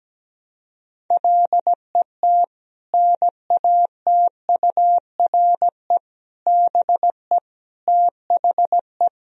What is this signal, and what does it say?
Morse code, heard as a keyed tone, about 17 wpm, 700 Hz. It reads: LET NATURE BE THE
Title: Sentences from Top 1000 Words - Review Part 3 - 17wpm